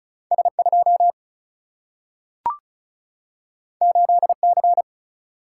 Morse code, heard as a keyed tone, about 36 wpm, 700 Hz. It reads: S2 8C
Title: Single-Letter Single-Number 35wpm